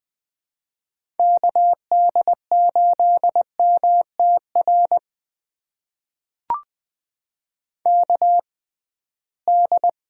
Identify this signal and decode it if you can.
Morse code, heard as a keyed tone, about 20 wpm, spaced as for 15 wpm, 700 Hz. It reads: KD8MTR KD
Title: USA Call Signs - 6 Characters - 6x Character Spacing - 20wpm